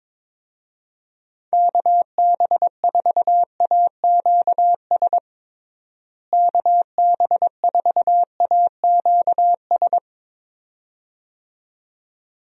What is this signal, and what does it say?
Morse code, heard as a keyed tone, about 22 wpm, 700 Hz. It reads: KB4AQS KB4AQS
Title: USA Call Signs - 6 Characters - Repeated 2x - 22wpm